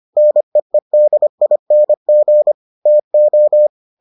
Morse code, heard as a keyed tone, about 25 wpm, 600 Hz. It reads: NEEDING TO